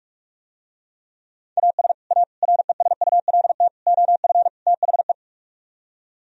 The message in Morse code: AS A RESULT OF THE

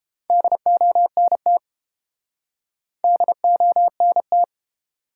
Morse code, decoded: DONT DONT

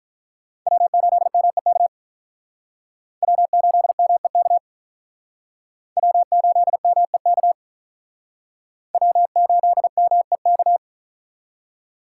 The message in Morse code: W8MEK W8MEK W8MEK W8MEK

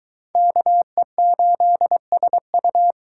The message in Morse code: KE8SU

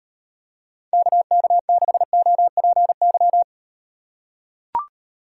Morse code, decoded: KK6OPY

